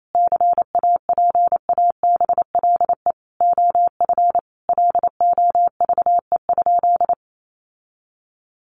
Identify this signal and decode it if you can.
Morse code, heard as a keyed tone, about 28 wpm, 700 Hz. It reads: CAPABLE OF LOVE?
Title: Sentences from Top 1200 Words 28wpm